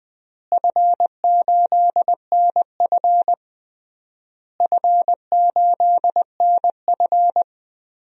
Morse code, heard as a keyed tone, about 20 wpm, 700 Hz. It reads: F8NF F8NF
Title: DXCC-100 Call Signs - 4 Characters - Repeated 3x - 20wpm